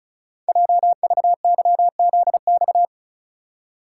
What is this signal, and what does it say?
Morse code, heard as a keyed tone, about 36 wpm, 700 Hz. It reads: JVYZX
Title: Code Groups - 5 Letters - 35wpm